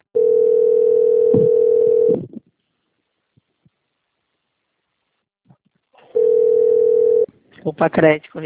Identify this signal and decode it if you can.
Ringback tone (North America), 440 + 480 Hz, with speech.
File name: out- - - - - .13686